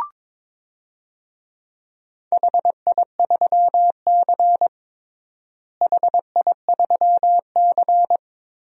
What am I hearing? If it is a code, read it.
Morse code, heard as a keyed tone, about 22 wpm, 700 Hz. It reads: HI3C HI3C
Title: DXCC-100 Call Signs - 4 Characters - Repeated 2x - 22wpm